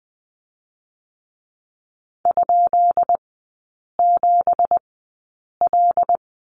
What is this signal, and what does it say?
Morse code, heard as a keyed tone, about 20 wpm, spaced as for 8 wpm, 700 Hz. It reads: ?7L
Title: 166 Lesson 27e - Number 7 - Three Characters 20wpm